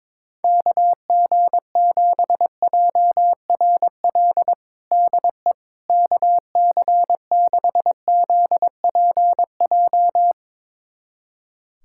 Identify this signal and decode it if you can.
Morse code, heard as a keyed tone, about 21 wpm, 700 Hz. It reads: KG7JRL DE KC6ZPJ
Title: USA Callsign DE Callsign - 6 Characters - 22wpm